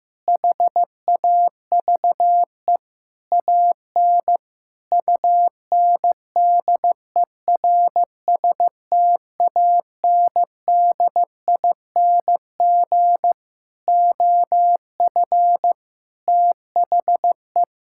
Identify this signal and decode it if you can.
Morse code, heard as a keyed tone, about 15 wpm, 700 Hz. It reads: HAVE AN UNDERSTANDING OF THE
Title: Sentences from Top 1000 Words - Review Part 5 - 15wpm